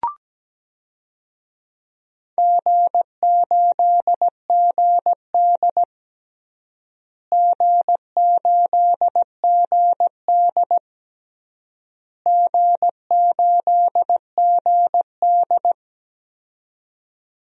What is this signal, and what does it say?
Morse code, heard as a keyed tone, about 17 wpm, 700 Hz. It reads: G8GD G8GD G8GD
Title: DXCC-100 Call Signs - 4 Characters - Repeated 3x - 17wpm